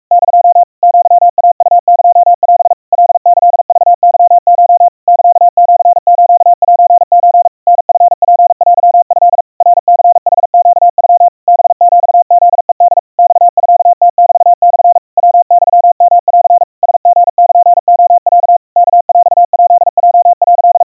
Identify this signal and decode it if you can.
Morse code (Japanese Wabun code), heard as a keyed tone, about 44 wpm, 700 Hz. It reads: <DO> アイウエオ カキクケコ サシスセソ タチツテト ナニヌネノ ハヒフヘホ マミムメモ ヤユヨン ラリルレロ ワヰヱヲ」
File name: 150 cpm 700 Hz Crispy